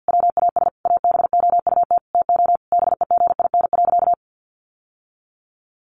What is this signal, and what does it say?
Morse code, heard as a keyed tone, about 50 wpm, 700 Hz. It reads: WAS ABOUT TO BEGIN.
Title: Sentences from Top 1200 Words 50wpm